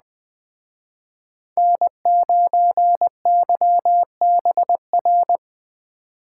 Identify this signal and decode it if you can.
Morse code, heard as a keyed tone, about 20 wpm, 700 Hz. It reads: N9YBR